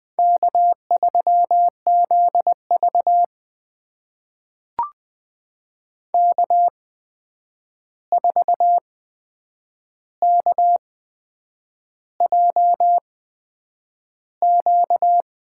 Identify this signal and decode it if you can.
Morse code, heard as a keyed tone, about 20 wpm, spaced as for 8 wpm, 700 Hz. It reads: K3ZV K4KJQ
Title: USA Call Signs - 8x Character Spacing - 20wpm